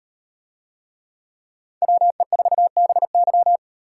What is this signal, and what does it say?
Morse code, heard as a keyed tone, about 38 wpm, 700 Hz. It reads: WE4BY